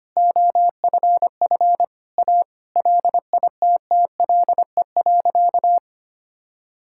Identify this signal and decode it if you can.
Morse code, heard as a keyed tone, about 25 wpm, 700 Hz. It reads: OFF A LITTLE.
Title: Sentences from Top 1000 Words - Review Part 6 - 25wpm